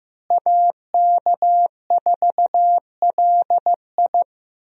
Morse code, heard as a keyed tone, about 15 wpm, 700 Hz. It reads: AK4LI
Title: USA Call Signs - 5 Characters - Repeated 3x - 15wpm